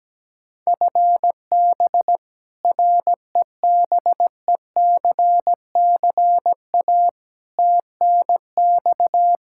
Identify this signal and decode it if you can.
Morse code, heard as a keyed tone, about 17 wpm, 700 Hz. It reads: FB REBECCA TNX
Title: Practice QSO - Exchange 3 - More Info - 17wpm